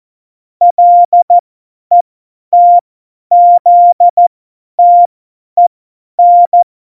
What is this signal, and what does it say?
Morse code, heard as a keyed tone, about 14 wpm, spaced as for 10 wpm, 700 Hz. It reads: LETZTEN
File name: Morsekurs / Witze / Witz 28. Mai 2019, 50 F70